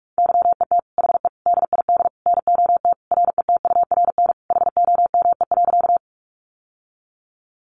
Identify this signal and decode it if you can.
Morse code, heard as a keyed tone, about 44 wpm, 700 Hz. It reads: YET HE DID NOT RETURN HOME.
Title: Sentences from Top 800 Words 45wpm